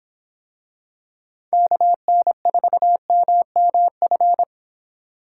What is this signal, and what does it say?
Morse code, heard as a keyed tone, about 24 wpm, 700 Hz. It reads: KN4MMF